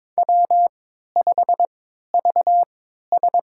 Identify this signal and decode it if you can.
Morse code, heard as a keyed tone, about 22 wpm, spaced as for 13 wpm, 700 Hz. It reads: W5VS
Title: USA Call Signs - 3x Character Spacing - 22wpm